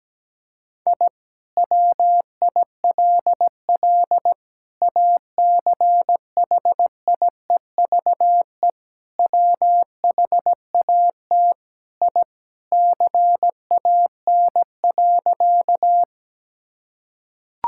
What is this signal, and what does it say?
Morse code, heard as a keyed tone, about 17 wpm, 700 Hz. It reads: I WILL ACHIEVE WHAT I CAN.
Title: Sentences from Top 1000 Words - Review Part 6 - 17wpm